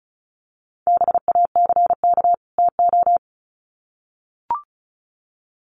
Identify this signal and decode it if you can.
Morse code, heard as a keyed tone, about 34 wpm, 700 Hz. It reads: BACK TO
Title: Sets of 2 Words 35wpm